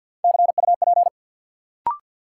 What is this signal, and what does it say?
Morse code, heard as a keyed tone, about 53 wpm, 700 Hz. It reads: CUP
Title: Sentences from Top 1600 Words - New Word Drill Alt - 50wpm